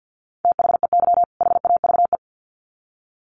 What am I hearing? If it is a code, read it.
Morse code, heard as a keyed tone, about 50 wpm, 700 Hz. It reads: THEY HAVE
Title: Sets of 2 Words 50wpm